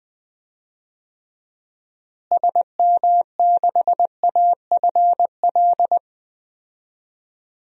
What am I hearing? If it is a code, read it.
Morse code, heard as a keyed tone, about 20 wpm, 700 Hz. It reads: SM6AFL
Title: DXCC-100 Call Signs - 6 Characters - Repeated 3x - 20wpm